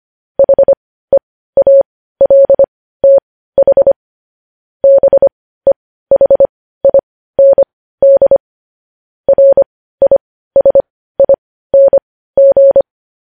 Morse code, heard as a keyed tone, about 25 wpm, spaced as for 15 wpm, 550 Hz. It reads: HEALTH BEHIND RISING
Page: news headlines in morse code at 15 wpm